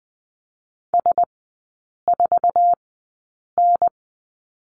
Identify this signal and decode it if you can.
Morse code, heard as a keyed tone, about 20 wpm, spaced as for 8 wpm, 700 Hz. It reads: S4N
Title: 047 Lesson 9e - Number 4 - Three Characters 20wpm